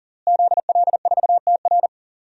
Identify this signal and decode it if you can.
Morse code, heard as a keyed tone, about 40 wpm, 700 Hz. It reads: ZL4TR
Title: DXCC-100 Call Signs - 5 Characters - Repeated 2x - 40wpm